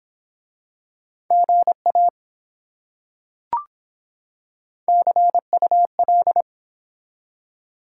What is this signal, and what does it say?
Morse code, heard as a keyed tone, about 24 wpm, 700 Hz. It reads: GA CUL